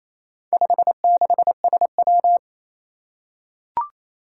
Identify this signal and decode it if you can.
Morse code, heard as a keyed tone, about 27 wpm, 700 Hz. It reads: 56SW
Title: Four Challenging Letter-Number Combinations 28wpm